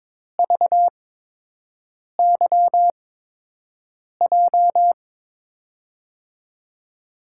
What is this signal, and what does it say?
Morse code, heard as a keyed tone, about 22 wpm, spaced as for 6 wpm, 700 Hz. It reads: VYJ
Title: USA Call Signs - 6 Characters - 8x Character Spacing - 22wpm